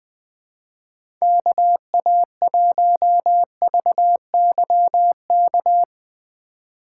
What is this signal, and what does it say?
Morse code, heard as a keyed tone, about 20 wpm, 700 Hz. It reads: KA1VYK
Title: USA Call Signs - 6 Characters - 3x Character Spacing - 20wpm